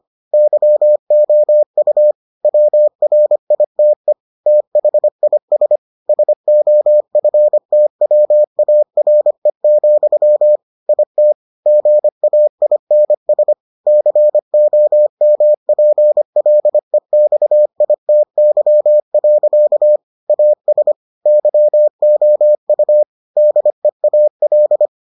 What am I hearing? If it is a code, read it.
Morse code, heard as a keyed tone, about 25 wpm, 600 Hz. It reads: YOU WRITE THIS SOFTWARE, IT GAINS COMPLEXITY. AS YOU DEAL